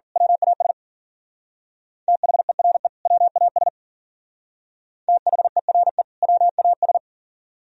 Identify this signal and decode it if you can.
Morse code, heard as a keyed tone, about 45 wpm, 700 Hz. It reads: WAS THERE WAS THERE WAS
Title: Speed-Racing - Sets of 2 Words 40wpm